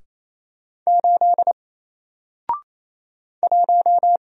Morse code, heard as a keyed tone, about 27 wpm, 700 Hz. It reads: 8 1
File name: Single-Letter-Number-28wpm